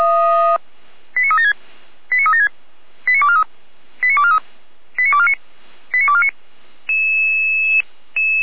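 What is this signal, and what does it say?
Five-tone selective call (ZVEI), heard as quick runs of five tones, tones of about 70 ms. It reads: 89376 89376 89243 89243 89249 89249
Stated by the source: ZVEI five-tone call 89310 received on SDR.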